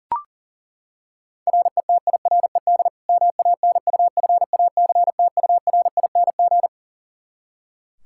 Morse code, heard as a keyed tone, about 40 wpm, 700 Hz. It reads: RETIRED MANUFACTURING
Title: K1USN SST QSO Practice 40wpm